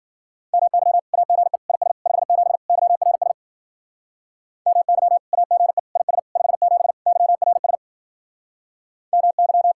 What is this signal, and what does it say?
Morse code, heard as a keyed tone, about 52 wpm, 700 Hz. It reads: MY AGE IS 57 YRS MY AGE IS 57 YRS MY